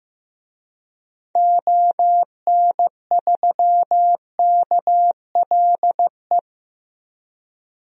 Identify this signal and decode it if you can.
Morse code, heard as a keyed tone, about 15 wpm, 700 Hz. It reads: ON3KLE
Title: DXCC-100 Call Signs - 6 Characters - Repeated 2x - 15wpm